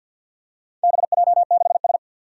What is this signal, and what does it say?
Morse code, heard as a keyed tone, about 50 wpm, 700 Hz. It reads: BJ6S
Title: Four Challenging Letter-Number Combinations 50wpm